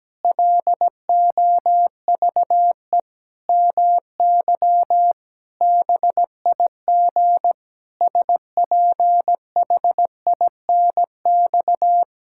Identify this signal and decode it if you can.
Morse code, heard as a keyed tone, about 17 wpm, 700 Hz. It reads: LOVE MY BIG SPHINX
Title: Pangram Challenge v2 - 17wpm